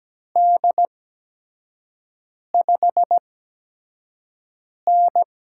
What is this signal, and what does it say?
Morse code, heard as a keyed tone, about 17 wpm, spaced as for 5 wpm, 700 Hz. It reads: D5N
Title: USA Call Signs - 4 Characters - 8x Character Spacing - 17wpm